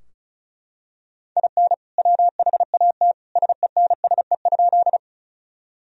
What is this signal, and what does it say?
Morse code, heard as a keyed tone, about 34 wpm, 700 Hz. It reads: IN WHAT SENSE?
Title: Sentences from Top 1000 Words - Review Part 4 - 35wpm